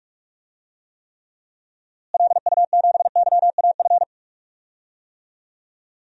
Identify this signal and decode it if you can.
Morse code, heard as a keyed tone, about 41 wpm, 700 Hz. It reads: LU7YAF
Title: DXCC-100 Call Signs - 6 Characters - Repeated 3x - 45wpm